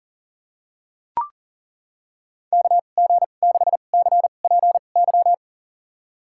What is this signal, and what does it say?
Morse code, heard as a keyed tone, about 40 wpm, spaced as for 29 wpm, 700 Hz. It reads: KG6CPY
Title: USA Call Signs - 6 Characters - 2x Character Spacing - 40wpm